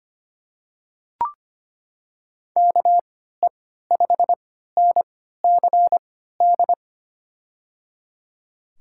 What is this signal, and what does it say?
Morse code, heard as a keyed tone, about 25 wpm, spaced as for 14 wpm, 700 Hz. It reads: KE5NCD
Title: USA Call Signs - 6 Characters - 3x Character Spacing - 25wpm